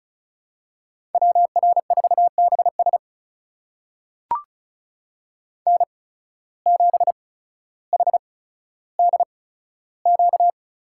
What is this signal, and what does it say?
Morse code, heard as a keyed tone, about 36 wpm, spaced as for 15 wpm, 700 Hz. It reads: WR4BS N7HDQ
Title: USA Call Signs - 5 Characters - 8x Character Spacing - 35wpm